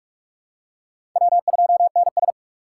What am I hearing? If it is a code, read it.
Morse code, heard as a keyed tone, about 44 wpm, 690 Hz. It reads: W2NS
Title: USA Call Signs - 4 Characters - 3x Character Spacing - 45wpm